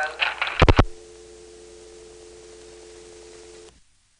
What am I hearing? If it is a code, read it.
Dial tone (North America), 350 + 440 Hz, with speech.
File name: phonetrips / PrinceWmCoVA5HQ